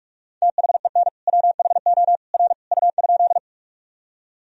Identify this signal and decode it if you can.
Morse code, heard as a keyed tone, about 42 wpm, 700 Hz. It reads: THEN WHO R U?
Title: Sentences from Top 100 Words and Abbreviations 45wpm